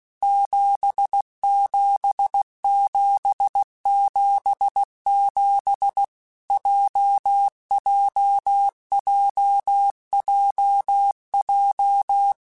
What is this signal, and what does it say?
Morse code, heard as a keyed tone, about 16 wpm, 800 Hz. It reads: 77777 JJJJJ
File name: Basic Letter Groups By7Groups / JZ17 / JZ17@15